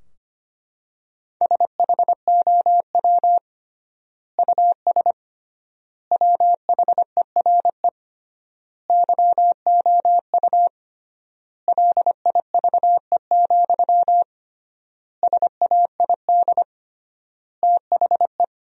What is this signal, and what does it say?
Morse code, heard as a keyed tone, about 25 wpm, 700 Hz. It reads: SHOW US WHERE YOU LIVE, SAID THE
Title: 3x Word-Spacing - Sentences from Top 200 Words - 25wpm